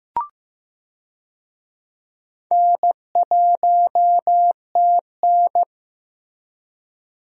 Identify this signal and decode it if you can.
Morse code, heard as a keyed tone, about 15 wpm, 700 Hz. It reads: N1TN
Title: USA Call Signs - 4 Characters - Repeated 2x - 15wpm